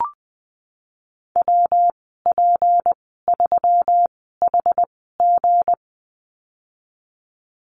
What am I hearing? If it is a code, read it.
Morse code, heard as a keyed tone, about 20 wpm, spaced as for 14 wpm, 700 Hz. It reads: WP3HG
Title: 161 Lesson 26g - Letter g - Call Signs 20wpm